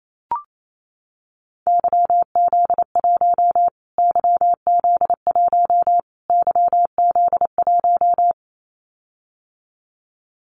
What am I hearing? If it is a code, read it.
Morse code, heard as a keyed tone, about 28 wpm, 700 Hz. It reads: YZ1 YZ1 YZ1
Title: Callsign Prefixes for Top 25 DXCC - Repeated 3x - 28wpm